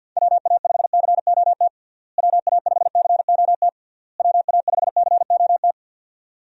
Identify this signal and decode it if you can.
Morse code, heard as a keyed tone, about 46 wpm, 700 Hz. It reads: WA5COT WA5COT WA5COT
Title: USA Call Signs - 6 Characters - Repeated 3x - 50wpm